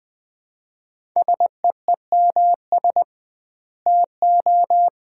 Morse code, heard as a keyed tone, about 20 wpm, 700 Hz. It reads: SEEMS TO
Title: 2x Word-Spacing - Sets of 3 Words - 20wpm